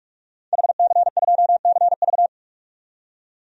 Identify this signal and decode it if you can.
Morse code, heard as a keyed tone, about 46 wpm, 700 Hz. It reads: HC2CV
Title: DXCC-100 Call Signs - 5 Characters - 45wpm